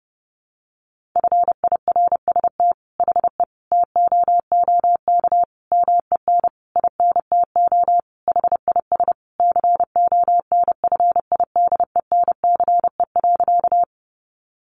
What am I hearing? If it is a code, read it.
Morse code, heard as a keyed tone, about 30 wpm, 700 Hz. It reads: FIRST HE TOOK MEN INTO HIS CONFIDENCE.
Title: Sentences from Top 1500 Words 30wpm